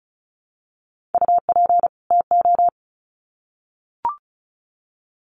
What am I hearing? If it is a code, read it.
Morse code, heard as a keyed tone, about 34 wpm, 700 Hz. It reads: UP TO
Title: Sets of 2 Words 35wpm